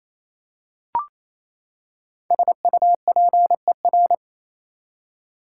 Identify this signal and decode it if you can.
Morse code, heard as a keyed tone, about 28 wpm, 700 Hz. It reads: SUPER
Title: Prefixes and Suffixes 28wpm